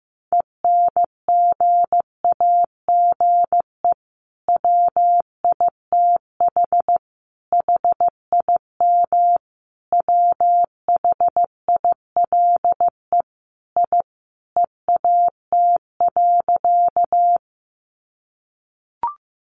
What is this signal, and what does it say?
Morse code, heard as a keyed tone, about 15 wpm, 700 Hz. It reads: ENGAGE WITH HIM WHILE I EAT.
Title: Sentences from Top 1500 Words 15wpm